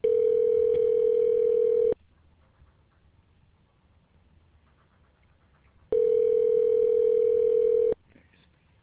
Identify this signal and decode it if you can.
Ringback tone (North America), 440 + 480 Hz.